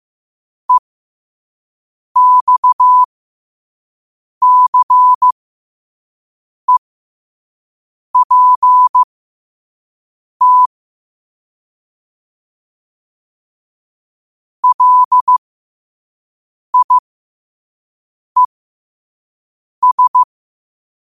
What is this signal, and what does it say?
Morse code, heard as a keyed tone, about 15 wpm, spaced as for 5 wpm, 1000 Hz. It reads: EXCEPT LIES